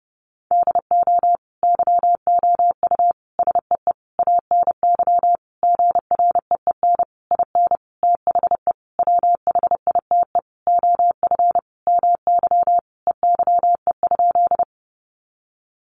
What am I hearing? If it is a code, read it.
Morse code, heard as a keyed tone, about 28 wpm, 700 Hz. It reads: DO YOU SEE ANY GREEN IN THE WHITE OF MY EYE?
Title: Sentences from Top 500 Words 30wpm